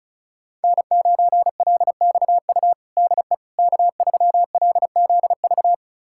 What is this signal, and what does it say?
Morse code, heard as a keyed tone, about 34 wpm, 690 Hz. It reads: N9LXU DE K3LZV
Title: USA Callsign DE Callsign - 5 Characters - 35wpm